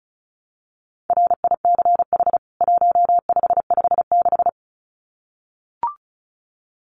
Morse code, heard as a keyed tone, about 34 wpm, 700 Hz. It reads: RICH 1556
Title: CWT Exchange 35wpm